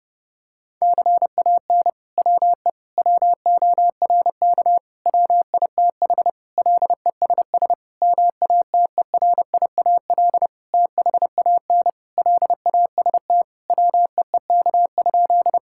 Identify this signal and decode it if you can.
Morse code, heard as a keyed tone, about 30 wpm, 700 Hz. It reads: CAN WE WORK WITH LESS MATERIAL THAN LAST WEEK?
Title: Sentences from Top 1000 Words - Review Part 5 - 30wpm